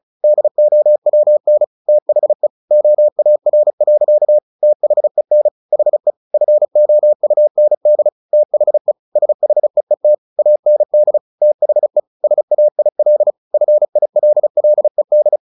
Morse code, heard as a keyed tone, about 34 wpm, 600 Hz. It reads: DOWN THE OAR. THEN HE FOUND THE SHEET AND THE SAIL FILLED
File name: the old man and the sea 2 35wpm